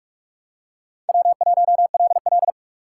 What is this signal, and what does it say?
Morse code, heard as a keyed tone, about 48 wpm, 700 Hz. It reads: W1LL
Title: USA Call Signs - 4 Characters - 3x Character Spacing - 45wpm